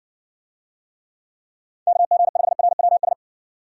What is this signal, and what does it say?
Morse code, heard as a keyed tone, about 65 wpm, 700 Hz. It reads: KG5RWS